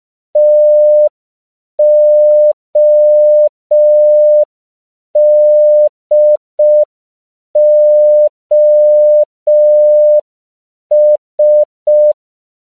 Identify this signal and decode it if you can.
Morse code, heard as a keyed tone, about 5 wpm, 600 Hz. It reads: TODOS
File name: CW / 5PPM